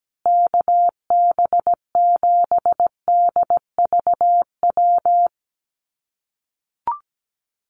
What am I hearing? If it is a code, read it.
Morse code, heard as a keyed tone, about 17 wpm, 700 Hz. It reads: KB7DVW
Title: USA Call Signs Repeated 3x 17wpm